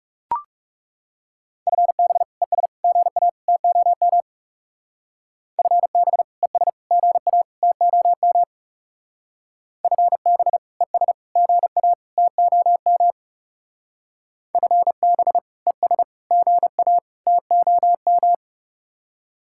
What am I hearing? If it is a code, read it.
Morse code, heard as a keyed tone, about 38 wpm, 700 Hz. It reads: FB ES GA TOM FB ES GA TOM FB ES GA TOM FB ES GA TOM
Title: Speed-Racing Common QSO Phrases - 30wpm